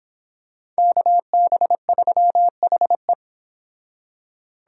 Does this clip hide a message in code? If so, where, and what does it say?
Yes: Morse code from 0.8 to 3.1 s, heard as a keyed tone, about 26 wpm, 700 Hz. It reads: KB3HE